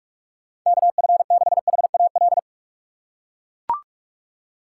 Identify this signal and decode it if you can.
Morse code, heard as a keyed tone, about 43 wpm, 700 Hz. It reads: KF6HAL
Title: USA Call Signs - 6 Characters - 6x Character Spacing - 45wpm